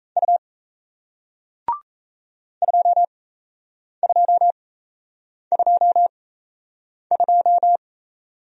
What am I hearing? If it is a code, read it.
Morse code, heard as a keyed tone, about 38 wpm, spaced as for 8 wpm, 700 Hz. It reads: U 2222